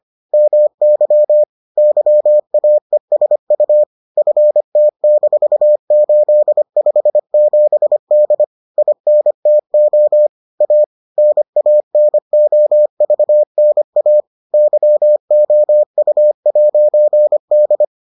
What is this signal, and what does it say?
Morse code, heard as a keyed tone, about 25 wpm, 600 Hz. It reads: MY YAESU FT-857D INTO A NANOVNA YOU'D